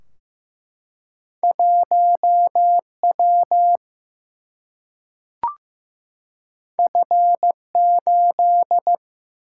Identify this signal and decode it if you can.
Morse code, heard as a keyed tone, about 15 wpm, 700 Hz. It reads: 1W F8
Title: Single-Letter Single-Number 15wpm